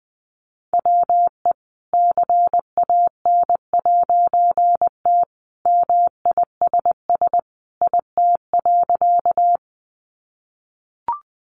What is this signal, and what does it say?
Morse code, heard as a keyed tone, about 20 wpm, 700 Hz. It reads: WE CAN'T MISS IT.
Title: A Sound Of Thunder 20wpm